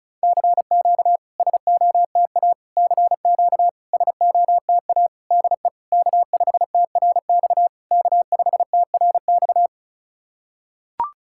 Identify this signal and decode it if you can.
Morse code, heard as a keyed tone, about 36 wpm, 700 Hz. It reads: CQ SOTA CQ SOTA DE K5TRX K5TRX
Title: SOTA QSO Practice 35wpm